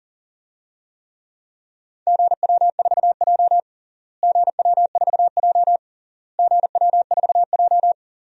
Morse code, heard as a keyed tone, about 40 wpm, 700 Hz. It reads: GW4J GW4J GW4J
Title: DXCC-100 Call Signs - 4 Characters - Repeated 3x - 40wpm